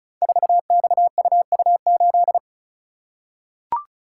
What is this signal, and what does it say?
Morse code, heard as a keyed tone, about 36 wpm, 700 Hz. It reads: 4XUU8
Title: Code Groups - 5 Letter:Numbers - 35wpm